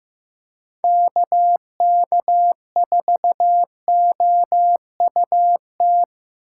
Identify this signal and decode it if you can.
Morse code, heard as a keyed tone, about 18 wpm, 700 Hz. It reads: KK4OUT
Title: Speed Racing - USA 2x3 Call Signs 15wpm